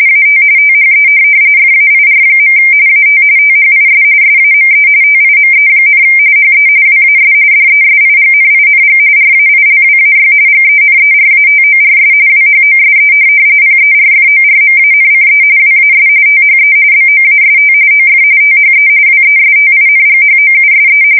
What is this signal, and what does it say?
RTTY, about 45.45 baud. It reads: AT THE FAIR. AT THIS SAME TIME, HE WAS DESIGNING RAILWAY GENERATORS WHOSE PERFORMANCE WAS THE BOAST OF THE WESTINGHOUSE